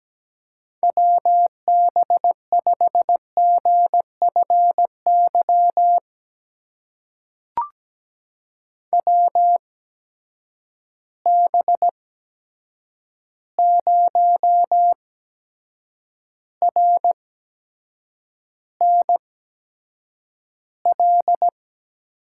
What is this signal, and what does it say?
Morse code, heard as a keyed tone, about 17 wpm, spaced as for 7 wpm, 700 Hz. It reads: WB5GFY WB0RNL
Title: USA Call Signs - 6 Characters - 8x Character Spacing - 17wpm